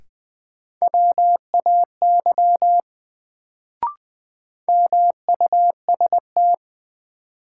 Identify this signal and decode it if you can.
Morse code, heard as a keyed tone, about 21 wpm, 700 Hz. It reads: WAY MUST
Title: Sets of 2 Words - New Word Drill - 20wpm